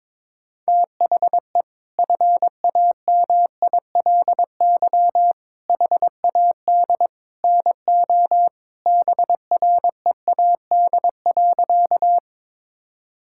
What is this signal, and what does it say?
Morse code, heard as a keyed tone, about 21 wpm, 700 Hz. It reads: THE FAMILY HAD NO BREAD.
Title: Sentences from Top 1600 Words 22wpm